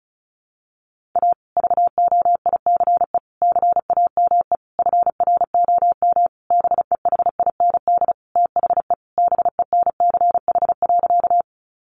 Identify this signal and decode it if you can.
Morse code, heard as a keyed tone, about 34 wpm, 700 Hz. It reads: A VOICE CAME FROM BEHIND THE BENCH.
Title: Sentences from Top 1400 Words 35wpm